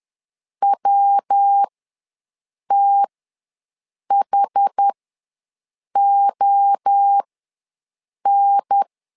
Morse code, heard as a keyed tone, about 10 wpm, spaced as for 6 wpm, 800 Hz. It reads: WTHON
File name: LEZIONI / 3 MWRO / 3 MWRO ws 10wpm 2310